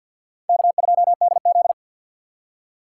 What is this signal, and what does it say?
Morse code, heard as a keyed tone, about 52 wpm, 690 Hz. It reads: K2DZ
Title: USA Call Signs - 2x Character Spacing - 50wpm